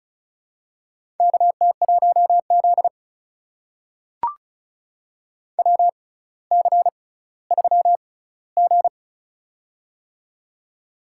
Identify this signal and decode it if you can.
Morse code, heard as a keyed tone, about 36 wpm, spaced as for 18 wpm, 700 Hz. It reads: KT1Z WC3G
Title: USA Call Signs - 4 Characters - 6x Character Spacing - 35wpm